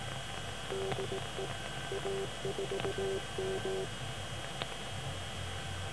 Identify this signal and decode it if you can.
Morse code, heard as a keyed tone, about 18 wpm, 390 Hz, with noise. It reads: DE A4M